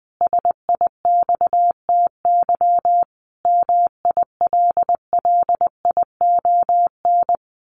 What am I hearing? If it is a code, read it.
Morse code, heard as a keyed tone, about 20 wpm, 700 Hz. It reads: SIXTY MILLION
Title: A Sound Of Thunder 20wpm